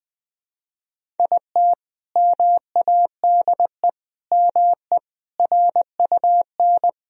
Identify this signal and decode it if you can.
Morse code, heard as a keyed tone, about 20 wpm, 700 Hz. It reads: IT MADE ME RUN
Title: Sentences from Top 1000 Words - Review Part 1 - 20wpm